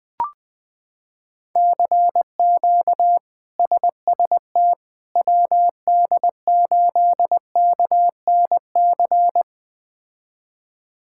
Morse code, heard as a keyed tone, about 20 wpm, 700 Hz. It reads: CQ SST WD8KNC
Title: K1USN SST QSO Practice - 20wpm